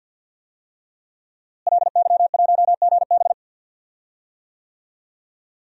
Morse code, heard as a keyed tone, about 50 wpm, 700 Hz. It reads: LY1GB
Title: DXCC-100 Call Signs - 5 Characters - Repeated 3x - 50wpm